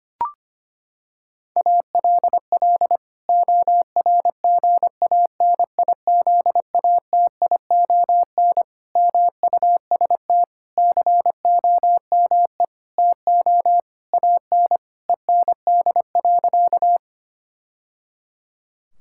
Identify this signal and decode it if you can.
Morse code, heard as a keyed tone, about 25 wpm, 700 Hz. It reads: ALL ORGANIZATION MUST COME TO AN END.
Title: Sentences from Top 1000 Words - Review Part 5 - 25wpm